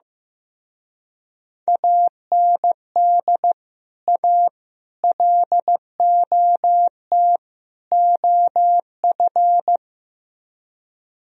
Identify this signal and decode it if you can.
Morse code, heard as a keyed tone, about 16 wpm, 700 Hz. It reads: AND A LOT OF
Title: Speed-Racing - Sets of 4 Words 15wpm